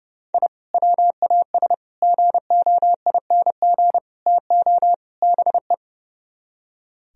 Morse code, heard as a keyed tone, about 36 wpm, 700 Hz. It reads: I WAS GOING TO BE
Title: Speed-Racing - Sets of 5 Words 30wpm